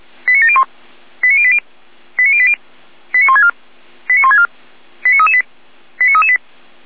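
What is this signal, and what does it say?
Five-tone selective call (ZVEI), heard as quick runs of five tones, tones of about 70 ms. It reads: 89701 89080 89080 89164 89164 89338 89338
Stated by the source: ZVEI five-tone call received on SDR.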